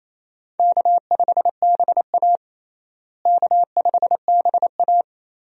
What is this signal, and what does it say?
Morse code, heard as a keyed tone, about 28 wpm, 700 Hz. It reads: K5BA K5BA
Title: USA Call Signs - 4 Characters - Repeated 2x - 28wpm